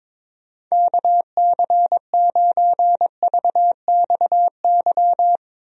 Morse code, heard as a keyed tone, about 21 wpm, 700 Hz. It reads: KC9VXY
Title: USA Call Signs - 6 Characters - Repeated 2x - 22wpm